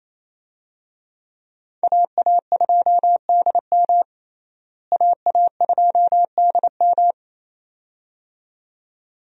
Morse code, heard as a keyed tone, about 27 wpm, 700 Hz. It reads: AA2DM AA2DM
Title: USA Call Signs - 5 Characters - Repeated 2x - 28wpm